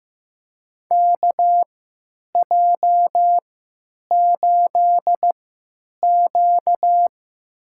Morse code, heard as a keyed tone, about 15 wpm, spaced as for 8 wpm, 700 Hz. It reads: KJ8Q
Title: USA Call Signs - 3x Character Spacing - 15wpm